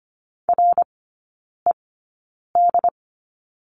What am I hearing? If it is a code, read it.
Morse code, heard as a keyed tone, about 25 wpm, spaced as for 9 wpm, 700 Hz. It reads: RED
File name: Lesson 14e - Number 2 - Three Characters 25wpm